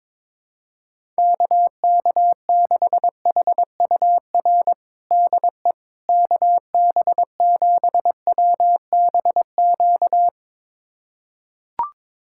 Morse code, heard as a keyed tone, about 21 wpm, 700 Hz. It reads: KK6HUR DE KB7WBQ
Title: USA Callsign DE Callsign - 6 Characters - 22wpm